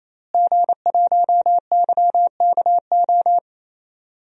Morse code, heard as a keyed tone, about 28 wpm, 700 Hz. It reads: G1YKO
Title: DXCC-100 Call Signs - 5 Characters - Repeated 3x - 28wpm